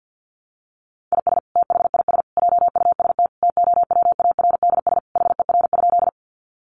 Morse code, heard as a keyed tone, about 48 wpm, 700 Hz. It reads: IS THIS JUST TOWARDS HER?
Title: Sentences from Top 1100 Words 50wpm